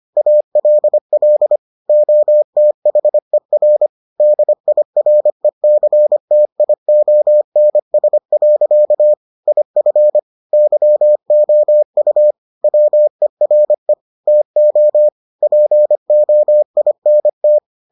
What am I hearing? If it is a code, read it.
Morse code, heard as a keyed tone, about 25 wpm, 600 Hz. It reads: ALL OTHER DIRECTIONS. IF YOU WERE TO POINT